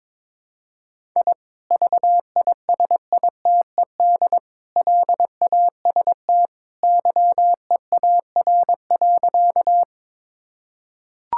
Morse code, heard as a keyed tone, about 22 wpm, 700 Hz. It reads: I VISITED LAST YEAR.
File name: practice / 22wpm / Sentences with 2-Word Phrases - 22wpm